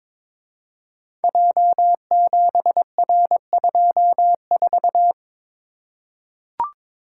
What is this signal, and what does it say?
Morse code, heard as a keyed tone, about 22 wpm, 700 Hz. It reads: J7R24
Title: Code Groups - 5 Letter:Numbers - 22wpm